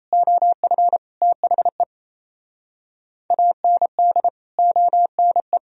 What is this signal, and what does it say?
Morse code, heard as a keyed tone, about 36 wpm, 700 Hz. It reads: OF THE AND ONE